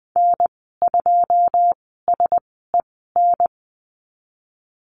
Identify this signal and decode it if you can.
Morse code, heard as a keyed tone, about 20 wpm, spaced as for 15 wpm, 700 Hz. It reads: N2SEN
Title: 088 Lesson 15g - Number 5 - Call Signs 20wpm